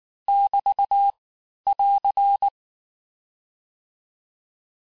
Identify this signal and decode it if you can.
Morse code, heard as a keyed tone, about 19 wpm, 800 Hz. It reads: <BT> <AR>
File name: cours-du-lundi files / cours-1589-texte-clair-vitesse-18mpm-puis-23mpm